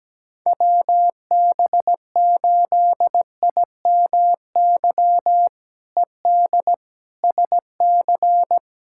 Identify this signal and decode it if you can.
Morse code, heard as a keyed tone, about 17 wpm, 700 Hz. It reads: WB8IMY ED SC
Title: K1USN SST QSO Practice - 17wpm